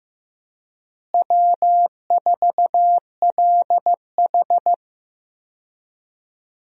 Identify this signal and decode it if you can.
Morse code, heard as a keyed tone, about 15 wpm, 700 Hz. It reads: W4LH
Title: USA Call Signs - 4 Characters - Repeated 2x - 15wpm